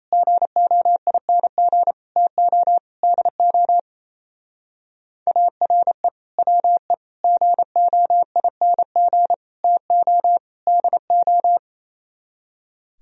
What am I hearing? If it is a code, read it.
Morse code, heard as a keyed tone, about 33 wpm, 700 Hz. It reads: GOING TO DO ARE WE GOING TO DO
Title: Speed-Racing - Sets of 5 Words 28wpm